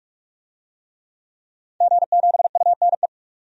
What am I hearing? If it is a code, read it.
Morse code, heard as a keyed tone, about 46 wpm, 700 Hz. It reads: G7UNE
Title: DXCC-100 Call Signs - 5 Characters - 45wpm